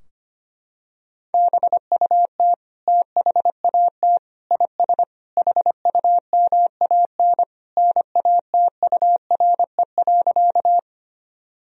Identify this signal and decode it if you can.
Morse code, heard as a keyed tone, about 25 wpm, 700 Hz. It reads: BUT THAT IS HUMAN NATURE.
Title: Sentences from Top 1000 Words - Review Part 3 - 25wpm